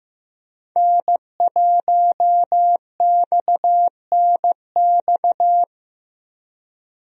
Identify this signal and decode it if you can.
Morse code, heard as a keyed tone, about 15 wpm, 700 Hz. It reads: N1XNX
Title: USA Call Signs - 5 Characters - Repeated 3x - 15wpm